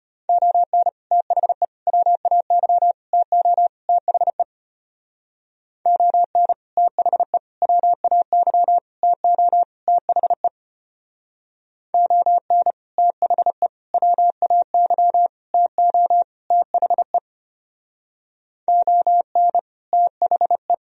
Morse code, heard as a keyed tone, about 31 wpm, 700 Hz. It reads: ON THE WAY TO THE ON THE WAY TO THE ON THE WAY TO THE ON THE
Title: Speed-Racing - Sets of 5 Words 25wpm